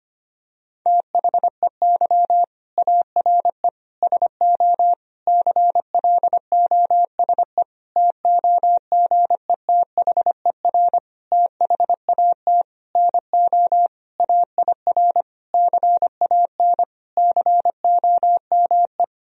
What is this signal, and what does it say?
Morse code, heard as a keyed tone, about 25 wpm, 700 Hz. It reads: THEY ARE SO CLOSE TOGETHER THAT NO AIR CAN COME
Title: Sentences from Top 1000 Words - Review Part 1 - 25wpm